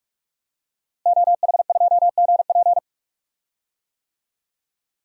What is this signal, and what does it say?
Morse code, heard as a keyed tone, about 44 wpm, 700 Hz. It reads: OH2GP
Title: DXCC-100 Call Signs - 5 Characters - Repeated 2x - 45wpm